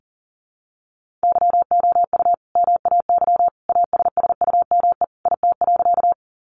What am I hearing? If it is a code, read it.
Morse code, heard as a keyed tone, about 40 wpm, 700 Hz. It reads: YOU MAY ASSUME IT.
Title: Sentences from Top 1500 Words 40wpm